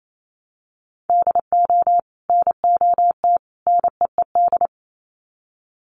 Morse code, heard as a keyed tone, about 28 wpm, 700 Hz. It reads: DO NOT NEED
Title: Sets of 3 Words 28wpm